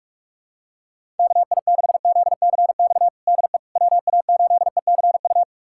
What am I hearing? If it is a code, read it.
Morse code, heard as a keyed tone, about 41 wpm, 700 Hz. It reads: KI6ZCX DE WA8ECU
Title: USA Callsign DE Callsign - 6 Characters - 45wpm